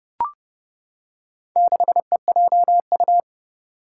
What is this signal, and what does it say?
Morse code, heard as a keyed tone, about 30 wpm, 700 Hz. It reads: 6EJU
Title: Four Challenging Letter-Number Combinations 30wpm